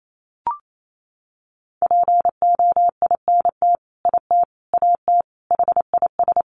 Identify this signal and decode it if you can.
Morse code, heard as a keyed tone, about 28 wpm, 700 Hz. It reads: POINT IT AT HIS
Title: Sentences from Top 300 Words 28wpm